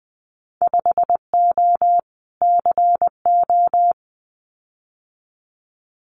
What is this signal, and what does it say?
Morse code, heard as a keyed tone, about 20 wpm, 700 Hz. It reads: 5O CO
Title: Winter Field Day Exchanges Encore 20wpm